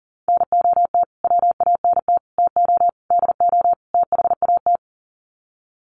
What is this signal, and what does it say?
Morse code, heard as a keyed tone, about 40 wpm, 700 Hz. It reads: NOT WANT TO DO THAT
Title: Sets of 5 Words Encore 40wpm